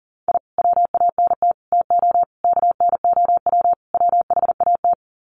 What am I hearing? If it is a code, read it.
Morse code, heard as a keyed tone, about 40 wpm, 700 Hz. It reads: I WANT TO KNOW WHAT
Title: Sets of 5 Words Encore2 40wpm